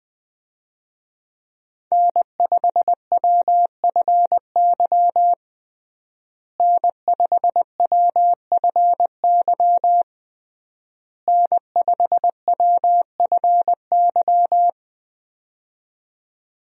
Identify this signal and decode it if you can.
Morse code, heard as a keyed tone, about 20 wpm, 700 Hz. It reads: N5WFY N5WFY N5WFY
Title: USA Call Signs - 5 Characters - Repeated 3x - 20wpm